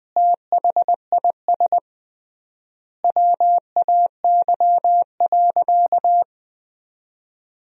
Morse code, heard as a keyed tone, about 20 wpm, 700 Hz. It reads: THIS WAY.
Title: 3x Word-Spacing - Sentences from Top 100 Words - 20wpm